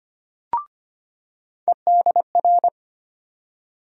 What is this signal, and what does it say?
Morse code, heard as a keyed tone, about 25 wpm, 700 Hz. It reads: EDR
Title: USA Call Sign Postfixes - 3 Letters 25wpm